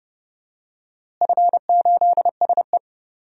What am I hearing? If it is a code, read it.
Morse code, heard as a keyed tone, about 31 wpm, 700 Hz. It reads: F8SE
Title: DXCC-100 Call Signs - 4 Characters - Repeated 2x - 30wpm